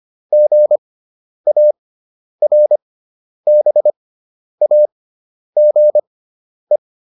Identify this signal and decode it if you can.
Morse code, heard as a keyed tone, about 25 wpm, spaced as for 10 wpm, 600 Hz. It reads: GARBAGE